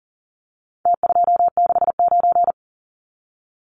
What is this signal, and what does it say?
Morse code, heard as a keyed tone, about 40 wpm, 700 Hz. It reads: T269
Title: Numbers - Contest Style - 4-Digit 40wpm